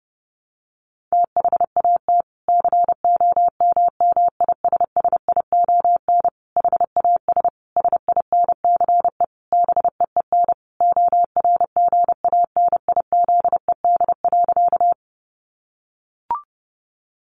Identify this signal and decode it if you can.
Morse code, heard as a keyed tone, about 28 wpm, 700 Hz. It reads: THAT COMMISSION HAS SINCE BEEN ORGANIZED.
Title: Sentences from Top 1400 Words 30wpm